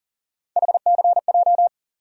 Morse code, heard as a keyed tone, about 40 wpm, 700 Hz. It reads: HCJ
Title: USA Call Sign Postfixes - 3 Letters 40wpm